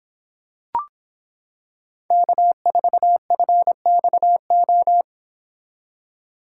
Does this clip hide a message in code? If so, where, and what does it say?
Yes: Morse code from 2.1 to 5.0 s, heard as a keyed tone, about 26 wpm, 700 Hz. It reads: K4FXO